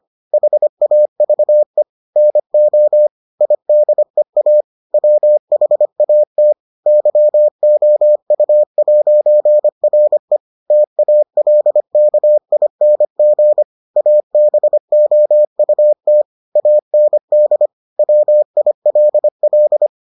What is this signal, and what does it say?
Morse code, heard as a keyed tone, about 25 wpm, 600 Hz. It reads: HAVE NO IDEA WHAT YOU'RE TALKING ABOUT AND WILL